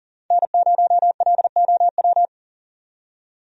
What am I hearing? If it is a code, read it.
Morse code, heard as a keyed tone, about 40 wpm, 700 Hz. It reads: N0LOW
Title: USA Call Signs - 5 Characters - 40wpm